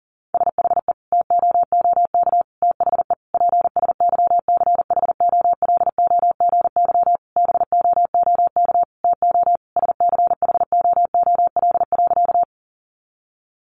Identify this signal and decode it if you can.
Morse code, heard as a keyed tone, about 40 wpm, 700 Hz. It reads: SHE TOOK THE PSYCHOLOGY BOOK TO SCHOOL.
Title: Sentences from Top 1400 Words 40wpm